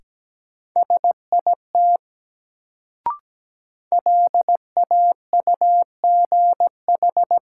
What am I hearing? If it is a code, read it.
Morse code, heard as a keyed tone, about 17 wpm, 700 Hz. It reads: SIT LAUGH
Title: Sentences from Top 500 Words - New Word Drill - 17wpm